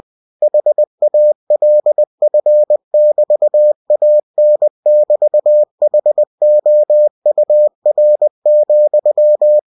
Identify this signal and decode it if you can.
Morse code, heard as a keyed tone, about 20 wpm, 600 Hz. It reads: HALF<BT>AN<BT>HOUR,
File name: cw-books / winnie-the-pooh / winnie-20wpm-page-0093